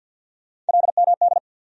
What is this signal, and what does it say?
Morse code, heard as a keyed tone, about 52 wpm, 700 Hz. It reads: LMD
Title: USA Call Sign Postfixes - 3 Letters 50wpm